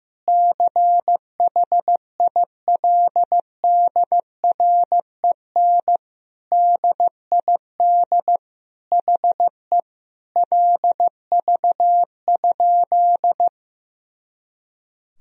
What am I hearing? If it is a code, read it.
Morse code, heard as a keyed tone, about 15 wpm, 700 Hz. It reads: CHILDREN DID HE LV?
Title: Sentences from Top 400 Words and Abbreviations 15wpm